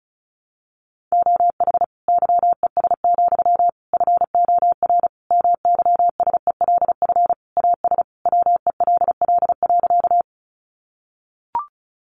Morse code, heard as a keyed tone, about 34 wpm, 700 Hz. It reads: OH YES, FOR MYSELF AS WELL.
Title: Sentences from Top 1300 Words 35wpm